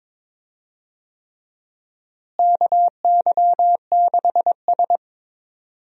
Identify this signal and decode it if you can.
Morse code, heard as a keyed tone, about 22 wpm, 700 Hz. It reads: KY6S
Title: USA Call Signs - 4 Characters - Repeated 3x - 22wpm